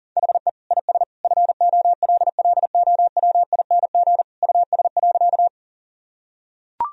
Morse code, heard as a keyed tone, about 40 wpm, 700 Hz. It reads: HE IS FOLLOWING US.
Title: Sentences from Top 1000 Words - Review Part 4 - 40wpm